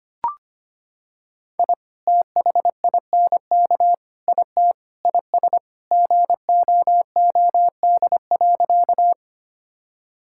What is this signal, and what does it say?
Morse code, heard as a keyed tone, about 25 wpm, 700 Hz. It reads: I THINK IT IS GOOD.
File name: practice / 25wpm / Sentences with 3-Word Phrases - 25wpm